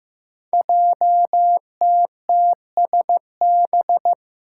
Morse code, heard as a keyed tone, about 15 wpm, 700 Hz. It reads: JTTSB
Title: Five Challenging Letter-Number Combinations 15wpm